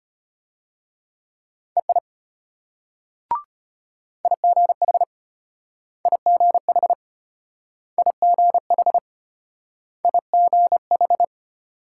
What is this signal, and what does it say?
Morse code, heard as a keyed tone, about 33 wpm, 700 Hz. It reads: EI IGH IGH IGH IGH